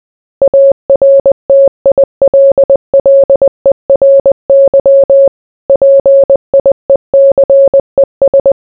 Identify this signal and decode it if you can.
Morse code, heard as a keyed tone, about 20 wpm, 550 Hz. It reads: ARTILLERY PIECES